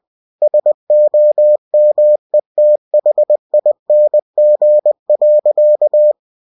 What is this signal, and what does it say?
Morse code, heard as a keyed tone, about 20 wpm, 600 Hz. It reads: SOMETHING.
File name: cw-books / winnie-the-pooh / winnie-20wpm-page-0049